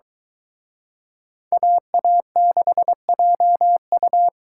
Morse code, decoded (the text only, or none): AA6JU